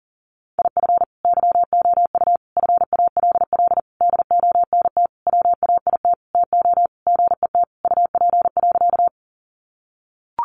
Morse code, heard as a keyed tone, about 40 wpm, 700 Hz. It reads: IF YOU FALL DONT WAIT TO GET UP.
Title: Sentences from Top 500 Words 40wpm